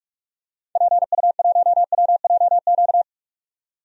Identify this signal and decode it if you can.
Morse code, heard as a keyed tone, about 42 wpm, 700 Hz. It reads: PU1WJQ
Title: DXCC-100 Call Signs - 6 Characters - 45wpm